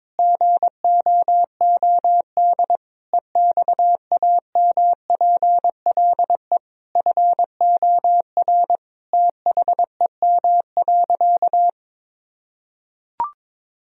Morse code, heard as a keyed tone, about 22 wpm, 700 Hz. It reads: GOOD EXAMPLE FOR THEM.
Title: Sentences from Top 1000 Words - Review Part 1 - 22wpm